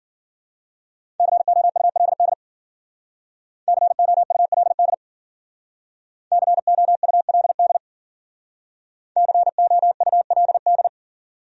Morse code, heard as a keyed tone, about 52 wpm, 700 Hz. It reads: COULD COULD COULD COULD